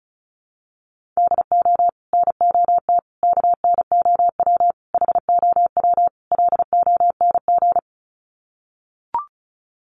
Morse code, heard as a keyed tone, about 34 wpm, 700 Hz. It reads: DO NOT KNOW HOW LONG
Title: Sets of 5 Words Encore2 35wpm